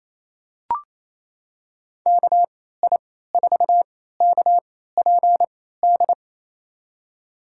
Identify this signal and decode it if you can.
Morse code, heard as a keyed tone, about 28 wpm, spaced as for 16 wpm, 700 Hz. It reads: KI4KPD
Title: USA Call Signs - 6 Characters - 3x Character Spacing - 28wpm